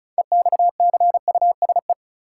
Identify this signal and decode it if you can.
Morse code, heard as a keyed tone, about 36 wpm, 700 Hz. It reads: EXCUSE
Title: Sentences from Top 700 Words - New Word Drill - 35wpm